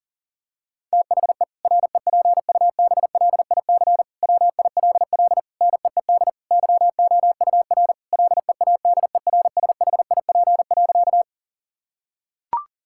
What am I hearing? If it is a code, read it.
Morse code, heard as a keyed tone, about 40 wpm, 700 Hz. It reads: THE REPUBLIC WILL NEED YOUR LEADERSHIP.
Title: Sentences from Top 1600 Words 40wpm